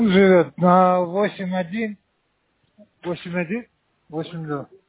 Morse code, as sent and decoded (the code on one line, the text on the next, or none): none